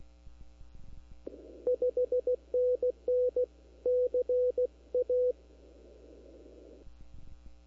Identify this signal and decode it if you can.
Morse code, heard as a keyed tone, about 16 wpm, 500 Hz. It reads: 5NN CA
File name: CW / 2005-k5tr arrl cw dx -23:00